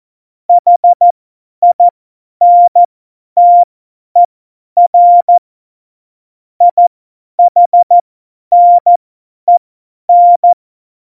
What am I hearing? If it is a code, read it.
Morse code, heard as a keyed tone, about 14 wpm, spaced as for 10 wpm, 700 Hz. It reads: HINTER IHNEN
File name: Morsekurs / Witze / Witz 05. Februar 2019, 50 F70